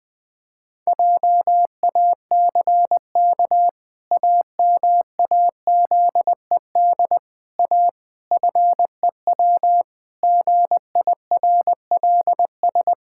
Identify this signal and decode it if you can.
Morse code, heard as a keyed tone, about 20 wpm, 700 Hz. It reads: JACK AMAZED A FEW GIRLS
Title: Pangram Challenge v2 - 20wpm